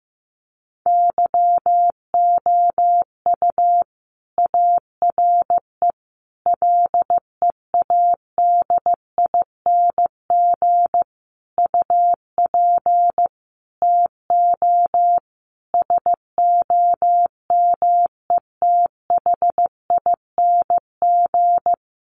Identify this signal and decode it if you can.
Morse code, heard as a keyed tone, about 15 wpm, 700 Hz. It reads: YOU ARE LEADING UP TO SOMETHING
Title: Sentences from Top 1100 Words 15wpm